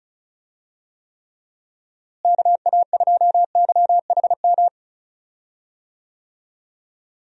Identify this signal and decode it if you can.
Morse code, heard as a keyed tone, about 34 wpm, 700 Hz. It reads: KA2YHM
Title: USA Call Signs - 6 Characters - Repeated 3x - 35wpm